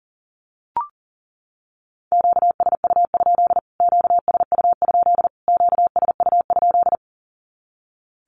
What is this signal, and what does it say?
Morse code, heard as a keyed tone, about 40 wpm, 700 Hz. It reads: QSU? QSU? QSU?
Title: Q-Codes Repeated 3x 40wpm